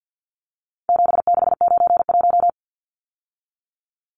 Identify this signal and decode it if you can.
Morse code, heard as a keyed tone, about 48 wpm, 700 Hz. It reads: 7691
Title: Numbers - 4-Digit 50wpm